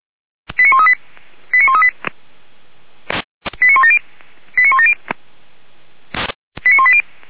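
Five-tone selective call (ZVEI), heard as quick runs of five tones, tones of about 70 ms. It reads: 89138 89138 89170 89170 89180
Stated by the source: ZVEI five-tone selective call received by SDR.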